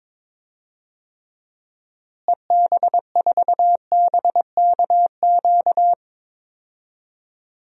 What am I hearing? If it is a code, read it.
Morse code, heard as a keyed tone, about 21 wpm, 700 Hz. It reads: EB4BKQ
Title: DXCC-100 Call Signs - 6 Characters - Repeated 2x - 22wpm